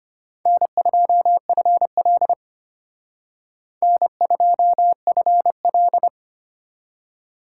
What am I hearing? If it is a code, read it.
Morse code, heard as a keyed tone, about 34 wpm, 700 Hz. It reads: N2FL N2FL